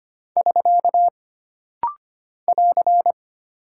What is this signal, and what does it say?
Morse code, heard as a keyed tone, about 25 wpm, 700 Hz. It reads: <SK> <AR>
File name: Single Punctuation-Prosign - Rapid-Fire - 25wpm